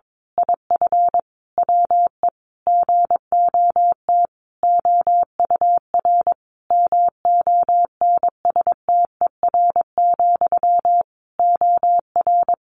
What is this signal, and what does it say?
Morse code, heard as a keyed tone, about 22 wpm, 700 Hz. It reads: IF WE GOT OUR MONSTER, OR
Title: A Sound Of Thunder 22wpm